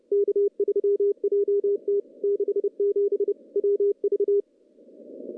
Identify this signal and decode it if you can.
Morse code, heard as a keyed tone, about 29 wpm, 410 Hz. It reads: K3JT 67 WV
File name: contests / CW-SS-k5tr- -0200